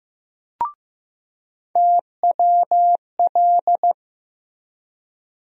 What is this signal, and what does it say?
Morse code, heard as a keyed tone, about 15 wpm, 700 Hz. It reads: TWL
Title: USA Call Sign Postfixes - 3 Letters 15wpm